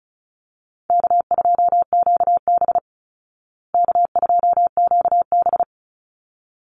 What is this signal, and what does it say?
Morse code, heard as a keyed tone, about 34 wpm, 700 Hz. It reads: K2QB K2QB
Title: Call Signs Repeated 3x - 35wpm